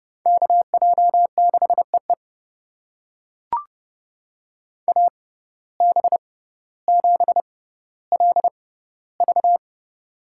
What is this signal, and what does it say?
Morse code, heard as a keyed tone, about 29 wpm, spaced as for 15 wpm, 700 Hz. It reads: KJ6EE AB7LV
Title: USA Call Signs - 5 Characters - 6x Character Spacing - 30wpm